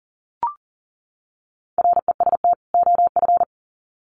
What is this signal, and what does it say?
Morse code, heard as a keyed tone, about 40 wpm, 700 Hz. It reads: REST OF
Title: Sets of 2 Words 40wpm